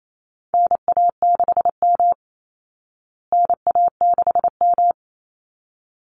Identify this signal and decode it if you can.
Morse code, heard as a keyed tone, about 28 wpm, 700 Hz. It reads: NA6M NA6M
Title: Call Signs Repeated 3x - 28wpm